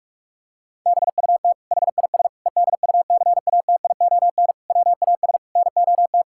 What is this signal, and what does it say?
Morse code, heard as a keyed tone, about 41 wpm, 700 Hz. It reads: BUT HIS EDUCATION WAS NOT
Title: Sentences from Top 1000 Words - Review Part 6 - 45wpm